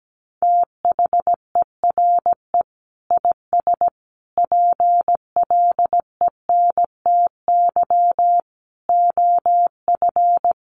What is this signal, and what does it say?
Morse code, heard as a keyed tone, about 17 wpm, 700 Hz. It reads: THERE IS PLENTY OF
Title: Sentences from Top 1300 Words 17wpm